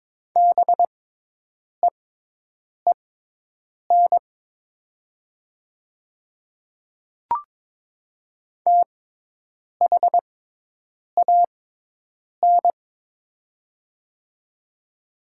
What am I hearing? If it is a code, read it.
Morse code, heard as a keyed tone, about 22 wpm, spaced as for 8 wpm, 700 Hz. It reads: BEEN THAN
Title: 6x Character Spacing - Top 300 Words - 22wpm